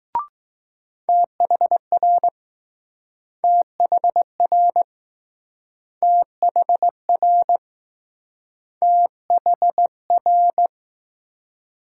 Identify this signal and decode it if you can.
Morse code, heard as a keyed tone, about 20 wpm, 700 Hz. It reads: THR THR THR THR